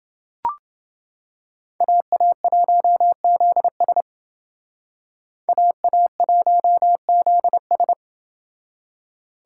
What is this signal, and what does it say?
Morse code, heard as a keyed tone, about 28 wpm, 700 Hz. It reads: AA1ZS AA1ZS